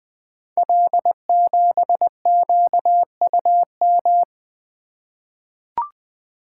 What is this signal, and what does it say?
Morse code, heard as a keyed tone, about 20 wpm, 700 Hz. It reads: L7QUM
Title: Code Groups - 5 Letter:Numbers - 20wpm